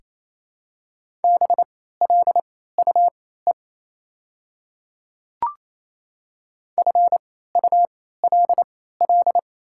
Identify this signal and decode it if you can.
Morse code, heard as a keyed tone, about 28 wpm, spaced as for 16 wpm, 700 Hz. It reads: BLUE FULL
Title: 3x Character Spacing - Top 500 Words - 28wpm